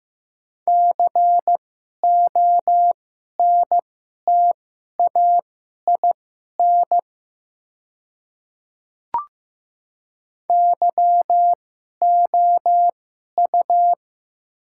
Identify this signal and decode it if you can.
Morse code, heard as a keyed tone, about 15 wpm, spaced as for 11 wpm, 700 Hz. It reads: CONTAIN YOU
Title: 2x Character Spacing - Top 500 Words - 15wpm